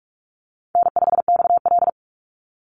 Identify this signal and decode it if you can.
Morse code, heard as a keyed tone, about 44 wpm, 700 Hz. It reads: N5XL
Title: NA SOTA Chaser Callsigns 45wpm